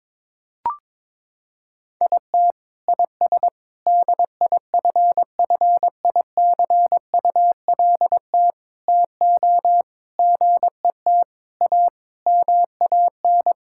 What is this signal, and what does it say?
Morse code, heard as a keyed tone, about 22 wpm, 700 Hz. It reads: IT IS DIFFICULT TO GET A MAN
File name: Sentences with 9-Word Phrases - 22wpm